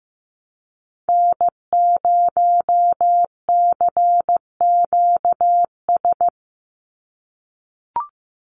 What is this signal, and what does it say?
Morse code, heard as a keyed tone, about 15 wpm, 700 Hz. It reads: N0CQS
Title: USA Call Signs 15wpm